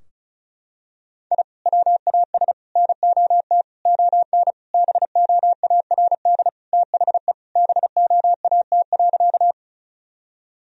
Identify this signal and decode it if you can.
Morse code, heard as a keyed tone, about 36 wpm, 700 Hz. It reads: I WAS NOT ON BOARD THE BOAT.
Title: Sentences from Top 1000 Words - Review Part 4 - 35wpm